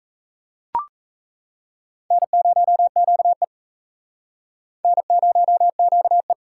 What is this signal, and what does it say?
Morse code, heard as a keyed tone, about 36 wpm, 700 Hz. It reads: N0QE N0QE